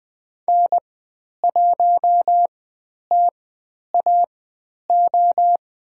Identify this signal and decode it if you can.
Morse code, heard as a keyed tone, about 20 wpm, spaced as for 10 wpm, 700 Hz. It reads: N1TAO
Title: 109 Lesson 18g - Letter m - Call Signs 20wpm